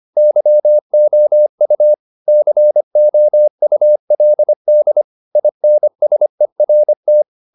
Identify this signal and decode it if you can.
Morse code, heard as a keyed tone, about 25 wpm, 600 Hz. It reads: YOU COULD INSERT